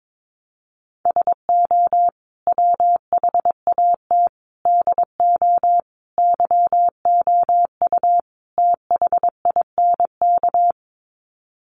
Morse code, heard as a keyed tone, about 22 wpm, 700 Hz. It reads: SO WHAT DO YOU THINK
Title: Sets of 5 Words Encore2 22wpm